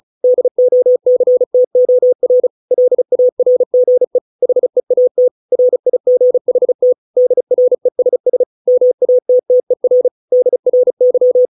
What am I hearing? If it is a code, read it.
Morse code, heard as a keyed tone, about 34 wpm, 500 Hz. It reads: DOCTOR LARGE HEAT RIGHT DRESS MATTER DRY